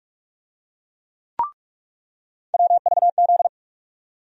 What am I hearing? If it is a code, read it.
Morse code, heard as a keyed tone, about 46 wpm, 700 Hz. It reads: WVZ
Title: USA Call Sign Postfixes - 3 Letters 45wpm